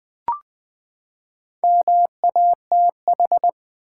Morse code, heard as a keyed tone, about 20 wpm, 700 Hz. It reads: MATH
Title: Four Letter Words - 20wpm